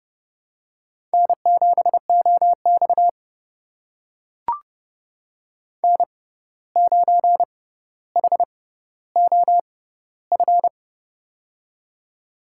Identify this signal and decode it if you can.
Morse code, heard as a keyed tone, about 30 wpm, spaced as for 14 wpm, 700 Hz. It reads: N7OX N9HOF
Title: USA Call Signs - 6x Character Spacing - 30wpm